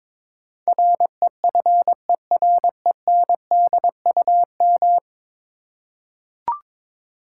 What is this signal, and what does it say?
Morse code, heard as a keyed tone, about 22 wpm, 700 Hz. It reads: REFERENDUM
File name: practice / 22wpm / 10 Letter Words - First 3 Letters Given - 22wpm